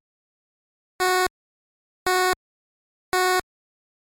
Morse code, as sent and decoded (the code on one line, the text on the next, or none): none